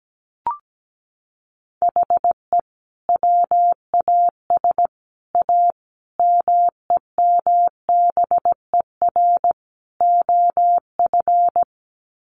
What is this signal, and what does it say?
Morse code, heard as a keyed tone, about 17 wpm, 700 Hz. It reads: HE WAS A MEMBER OF
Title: Sets of 5 Words Encore2 17wpm